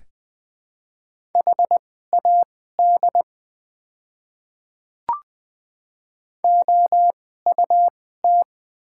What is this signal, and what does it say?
Morse code, heard as a keyed tone, about 20 wpm, spaced as for 14 wpm, 700 Hz. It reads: HAD OUT
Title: 2x Character Spacing - Top 200 Words - 20wpm